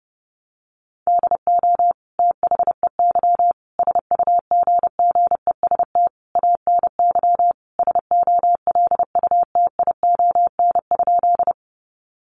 Morse code, heard as a keyed tone, about 30 wpm, 700 Hz. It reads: DO THEY SUGGEST ANY SOLUTION?
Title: Sentences from Top 1300 Words 30wpm